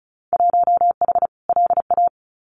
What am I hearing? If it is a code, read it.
Morse code, heard as a keyed tone, about 34 wpm, 700 Hz. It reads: 1H LA
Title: Winter Field Day Exchanges 35wpm